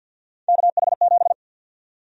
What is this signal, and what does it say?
Morse code, heard as a keyed tone, about 52 wpm, 700 Hz. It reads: KH7
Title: USA Call Sign Prefixes - 50wpm